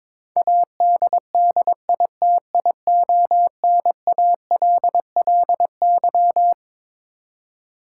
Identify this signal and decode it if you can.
Morse code, heard as a keyed tone, about 22 wpm, 700 Hz. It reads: ADDITIONALLY
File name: Twelve Letter Words - 22wpm